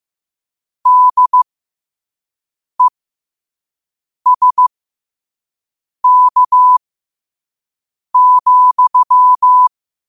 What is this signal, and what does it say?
Morse code, heard as a keyed tone, about 15 wpm, spaced as for 5 wpm, 1000 Hz. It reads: DESK,
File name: qotdfiles / 5wpm